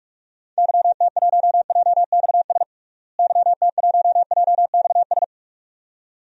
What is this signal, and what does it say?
Morse code, heard as a keyed tone, about 46 wpm, 700 Hz. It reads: YT1JXS YT1JXS
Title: DXCC-100 Call Signs - 6 Characters - Repeated 2x - 45wpm